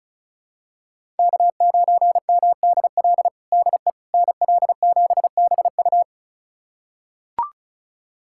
Morse code, heard as a keyed tone, about 36 wpm, 700 Hz. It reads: K9MDL DE NL7BU
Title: USA Callsign DE Callsign - 5 Characters - 35wpm